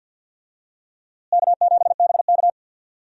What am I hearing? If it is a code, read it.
Morse code, heard as a keyed tone, about 47 wpm, 700 Hz. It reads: K7BK
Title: USA Call Signs - 4 Characters - Repeated 3x - 50wpm